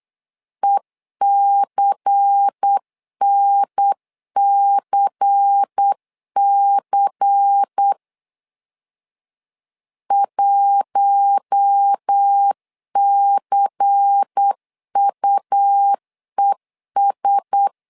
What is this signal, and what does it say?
Morse code, heard as a keyed tone, about 8 wpm, 800 Hz. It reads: ECNCC 1CUES
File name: LEZIONI / 5 LBVC / 5 LBVC 8wpm 2935